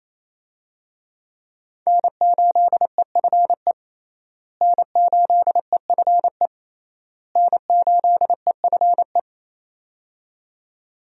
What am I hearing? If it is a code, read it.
Morse code, heard as a keyed tone, about 28 wpm, 700 Hz. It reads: N8EFE N8EFE N8EFE
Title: USA Call Signs - 5 Characters - Repeated 3x - 28wpm